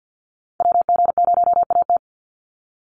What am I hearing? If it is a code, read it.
Morse code, heard as a keyed tone, about 50 wpm, 700 Hz. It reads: WG0AT
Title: NA SOTA Activator Callsigns 50wpm